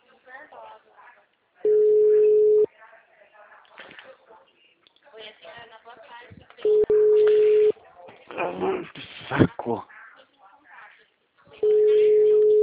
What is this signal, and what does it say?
Ringback tone (Europe, 425 Hz), with speech.